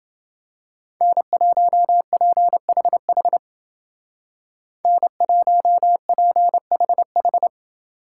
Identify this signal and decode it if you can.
Morse code, heard as a keyed tone, about 29 wpm, 700 Hz. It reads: N1PHH N1PHH